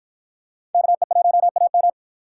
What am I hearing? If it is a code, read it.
Morse code, heard as a keyed tone, about 48 wpm, 700 Hz. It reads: KE1AM